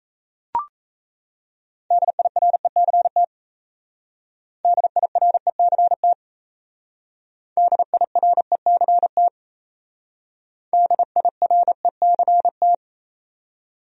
Morse code, heard as a keyed tone, about 36 wpm, 700 Hz. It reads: DIRECT DIRECT DIRECT DIRECT